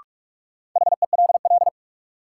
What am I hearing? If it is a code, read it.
Morse code, heard as a keyed tone, about 44 wpm, 700 Hz. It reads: HELL
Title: Sets of 5 Words Encore - New Word Drill - 45wpm